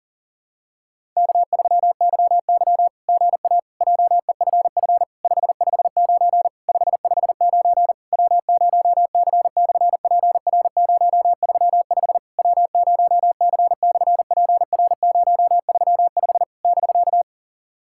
Morse code, heard as a keyed tone, about 40 wpm, 700 Hz. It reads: K3YY GA JEFF 559 559 W0C/PR035 W0C/PR035 <BK>
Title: SOTA QSO Practice 40wpm